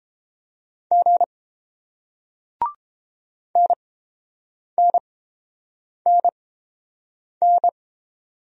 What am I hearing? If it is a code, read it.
Morse code, heard as a keyed tone, about 30 wpm, spaced as for 7 wpm, 700 Hz. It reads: G NNNN